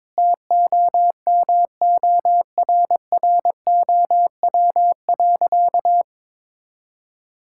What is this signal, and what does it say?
Morse code, heard as a keyed tone, about 22 wpm, 700 Hz. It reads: TOMORROW.
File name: practice / 22wpm / Sentences with 3-Word Phrases - 22wpm